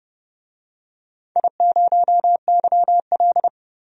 Morse code, heard as a keyed tone, about 30 wpm, 700 Hz. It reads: I0YL
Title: DXCC-100 Call Signs - 4 Characters - 30wpm